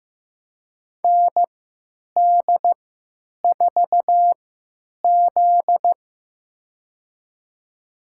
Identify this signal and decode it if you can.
Morse code, heard as a keyed tone, about 15 wpm, spaced as for 9 wpm, 700 Hz. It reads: ND4Z
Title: USA Call Signs - 4 Characters - 3x Character Spacing - 15wpm